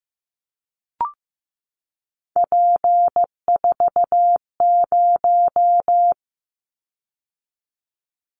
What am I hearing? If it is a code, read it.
Morse code, heard as a keyed tone, about 15 wpm, 700 Hz. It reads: P40
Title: Callsign Prefixes for Top 100 DXCC - 15wpm